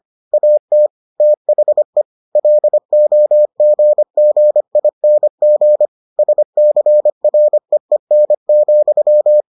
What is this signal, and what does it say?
Morse code, heard as a keyed tone, about 25 wpm, 600 Hz. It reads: AT THE LOGGING SCREEN,